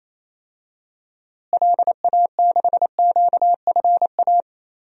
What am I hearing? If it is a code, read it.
Morse code, heard as a keyed tone, about 28 wpm, 700 Hz. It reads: LA6QFA
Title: DXCC-100 Call Signs - 6 Characters - Repeated 2x - 28wpm